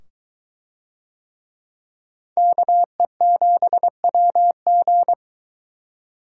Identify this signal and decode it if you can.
Morse code, heard as a keyed tone, about 22 wpm, 700 Hz. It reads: KE7WG